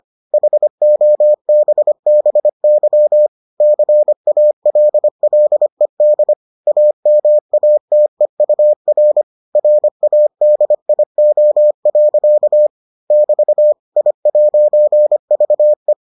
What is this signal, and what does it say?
Morse code, heard as a keyed tone, about 25 wpm, 600 Hz. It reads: HOBBY CALLED AMATEUR RADIO. <BT> I'VE